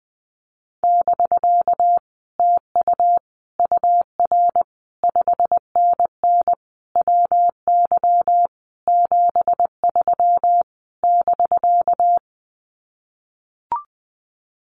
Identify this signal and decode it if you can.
Morse code, heard as a keyed tone, about 20 wpm, 700 Hz. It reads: <BK> TU UR 5NN WY 73 <BK>
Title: Common QSO Phrases Encore 20wpm